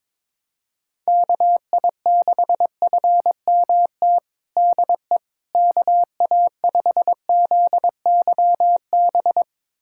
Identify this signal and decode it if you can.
Morse code, heard as a keyed tone, about 22 wpm, 700 Hz. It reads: KI6FMT DE KA5ZYB
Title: USA Callsign DE Callsign - 6 Characters - 22wpm